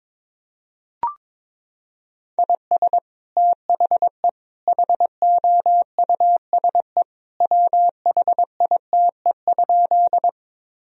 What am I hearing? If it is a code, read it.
Morse code, heard as a keyed tone, about 22 wpm, 700 Hz. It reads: IS THE HOUSE WHITE?
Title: Sentences from Top 1000 Words - Review Part 1 - 22wpm